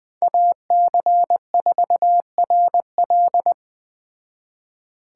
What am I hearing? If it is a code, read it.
Morse code, heard as a keyed tone, about 20 wpm, 700 Hz. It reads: AC4RL
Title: Speed Racing - USA 2x2 Call Signs 17wpm